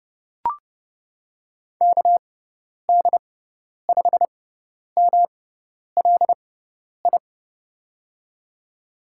Morse code, heard as a keyed tone, about 32 wpm, spaced as for 10 wpm, 700 Hz. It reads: KD5MLI
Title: USA Call Signs - 6 Characters - 6x Character Spacing - 30wpm